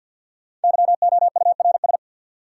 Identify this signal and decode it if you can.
Morse code, heard as a keyed tone, about 52 wpm, 700 Hz. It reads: YOURS
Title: Sentences from Top 1500 Words - New Word Drill Alt - 50wpm